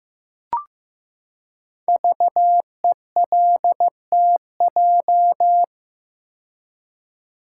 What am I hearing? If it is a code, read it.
Morse code, heard as a keyed tone, about 15 wpm, 700 Hz. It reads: VELTJ
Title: Code Groups - 5 Letter:Numbers - 15wpm